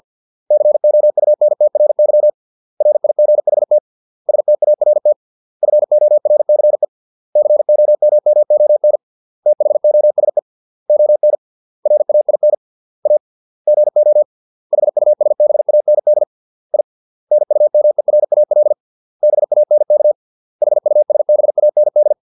Morse code, heard as a keyed tone, about 50 wpm, 600 Hz. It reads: COUNTRY RIGHT START FORCE COMMON THOSE ON RAIN A GO HUSBAND I NUMERAL BANK HUSBAND
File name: 632 words / 632 Words 50WPM